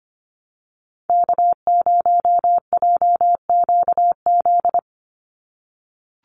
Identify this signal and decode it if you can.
Morse code, heard as a keyed tone, about 25 wpm, 700 Hz. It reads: K0JQZ
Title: NA SOTA Chaser Callsigns 25wpm